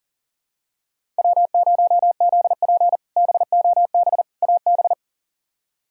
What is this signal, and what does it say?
Morse code, heard as a keyed tone, about 40 wpm, 700 Hz. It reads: W0ZP BOB AB
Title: K1USN SST QSO Practice - 40wpm